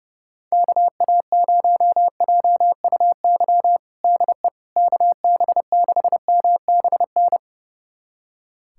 Morse code, heard as a keyed tone, about 30 wpm, 700 Hz. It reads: KA0JUY DE KB6MBN
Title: USA Callsign DE Callsign 30wpm